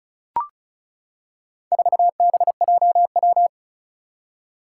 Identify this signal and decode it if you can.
Morse code, heard as a keyed tone, about 36 wpm, 700 Hz. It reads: 4BJW
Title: Four Challenging Letter-Number Combinations 35wpm